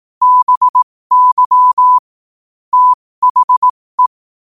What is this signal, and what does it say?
Morse code, heard as a keyed tone, about 18 wpm, spaced as for 15 wpm, 1000 Hz. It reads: BY THE
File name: qotdfiles / 15wpm